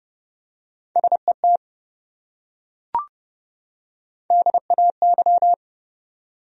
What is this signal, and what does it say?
Morse code, heard as a keyed tone, about 27 wpm, 700 Hz. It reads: SET DAY